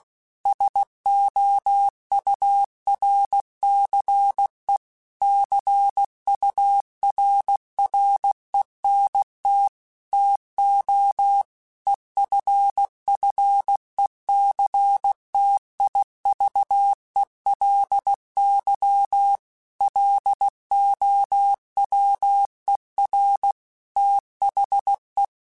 Morse code, heard as a keyed tone, about 16 wpm, 800 Hz. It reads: SOURCE CURRENT TO EFFECTIVELY LOWER THE
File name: Technical Subjects / Technical Subjects-mp3s / XC @15